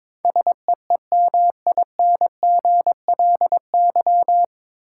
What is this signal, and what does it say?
Morse code, heard as a keyed tone, about 22 wpm, 700 Hz. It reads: SEEMINGLY
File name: 9 Letter Words - First 3 Letters Given - 22wpm